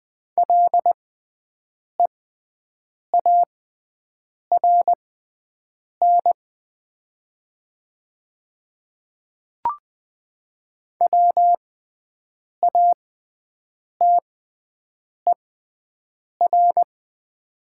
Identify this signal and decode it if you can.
Morse code, heard as a keyed tone, about 20 wpm, spaced as for 7 wpm, 700 Hz. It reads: LEARN WATER
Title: 6x Character Spacing - Top 300 Words - 20wpm